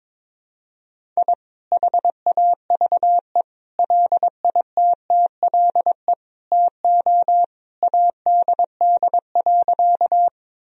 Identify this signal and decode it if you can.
Morse code, heard as a keyed tone, about 21 wpm, 700 Hz. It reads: I HAVE LITTLE TO ADD.
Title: Sentences from Top 1000 Words - Review Part 1 - 22wpm